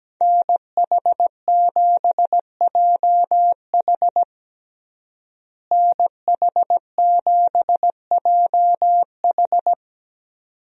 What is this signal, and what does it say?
Morse code, heard as a keyed tone, about 17 wpm, 700 Hz. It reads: NH7JH NH7JH
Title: USA Call Signs - 5 Characters - Repeated 2x - 17wpm